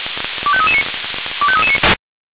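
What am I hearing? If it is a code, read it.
Five-tone selective call (ZVEI), heard as quick runs of five tones, tones of about 70 ms. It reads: 25339 25339
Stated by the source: ZVEI five-tone selective call received by SDR.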